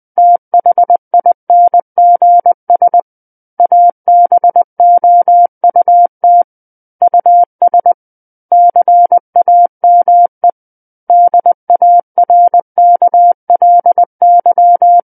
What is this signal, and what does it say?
Morse code, heard as a keyed tone, about 20 wpm, 700 Hz. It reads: THINGS ABOUT US CAME DARKLY
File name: 20 WPM